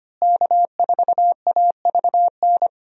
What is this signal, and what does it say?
Morse code, heard as a keyed tone, about 25 wpm, 700 Hz. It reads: K4AVN
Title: USA Call Signs - 5 Characters - Repeated 3x - 25wpm